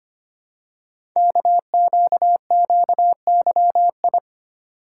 Morse code, heard as a keyed tone, about 25 wpm, 700 Hz. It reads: KQQYI
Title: Code Groups - 5 Letter:Numbers - 25wpm